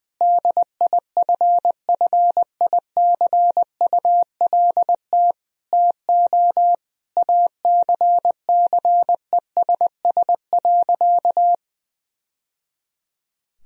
Morse code, heard as a keyed tone, about 20 wpm, 700 Hz. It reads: DIFFICULT TO ACCESS.
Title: Sentences from Top 1000 Words - Review Part 4 - 20wpm